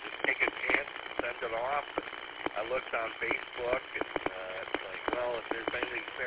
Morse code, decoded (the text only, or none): none